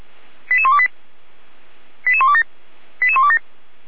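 Five-tone selective call (ZVEI), heard as quick runs of five tones, tones of about 70 ms. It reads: 88128 88127 88127
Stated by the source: ZVEI five-tone selective call 88129 received by SDR.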